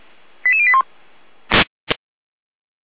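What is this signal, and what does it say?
Five-tone selective call (ZVEI), heard as quick runs of five tones, tones of about 70 ms. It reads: 88081
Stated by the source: ZVEI five-tone selective call received on an SDR.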